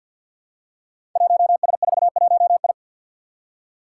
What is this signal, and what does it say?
Morse code, heard as a keyed tone, about 47 wpm, 700 Hz. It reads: 1S41I
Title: Five Challenging Letter-Number Combinations 50wpm